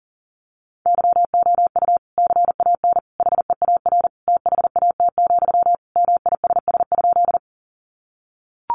Morse code, heard as a keyed tone, about 40 wpm, 700 Hz. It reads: YOU CAN HEAR THAT, MISS?
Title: Sentences from Top 500 Words 40wpm